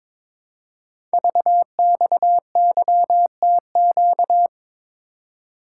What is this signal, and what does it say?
Morse code, heard as a keyed tone, about 22 wpm, 700 Hz. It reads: VXYTQ
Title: Code Groups - 5 Letter:Numbers - 22wpm